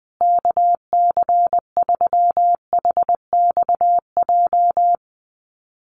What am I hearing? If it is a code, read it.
Morse code, heard as a keyed tone, about 20 wpm, 700 Hz. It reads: KC3HXJ
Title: USA Call Signs Repeated 3x 20wpm